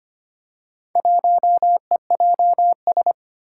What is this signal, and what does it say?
Morse code, heard as a keyed tone, about 25 wpm, 700 Hz. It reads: 1EJS
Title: Four Challenging Letter-Number Combinations 25wpm